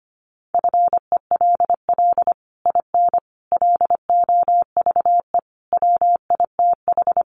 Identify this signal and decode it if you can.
Morse code, heard as a keyed tone, about 25 wpm, 700 Hz. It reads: FELL IN LOVE WITH
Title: Sets of 5 Words Encore2 25wpm